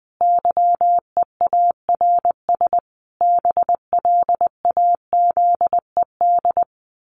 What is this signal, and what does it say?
Morse code, heard as a keyed tone, about 20 wpm, 700 Hz. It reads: YEARS BLAZED
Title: A Sound Of Thunder 20wpm